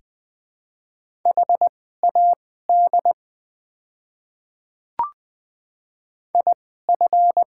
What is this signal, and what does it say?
Morse code, heard as a keyed tone, about 20 wpm, spaced as for 14 wpm, 700 Hz. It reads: HAD IF
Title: 2x Character Spacing - Top 300 Words - 20wpm